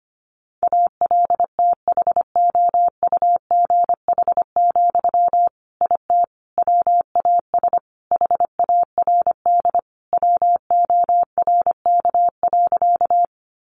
Morse code, heard as a keyed tone, about 25 wpm, 700 Hz. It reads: ALTHOUGH, IT WAS HARD WORK.
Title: Sentences from Top 600 Words 25wpm